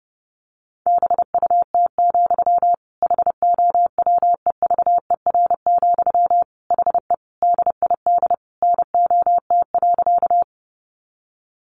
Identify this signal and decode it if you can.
Morse code, heard as a keyed tone, about 30 wpm, 700 Hz. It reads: BUT, HOWEVER, HE DID NOT.
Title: Sentences from Top 600 Words 30wpm